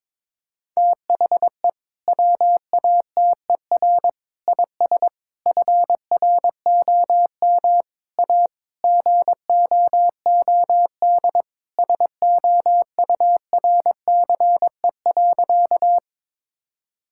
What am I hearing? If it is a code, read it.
Morse code, heard as a keyed tone, about 22 wpm, 700 Hz. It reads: THE WATER IS FROM A GOOD SOURCE.
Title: Sentences from Top 1000 Words - Review Part 5 - 22wpm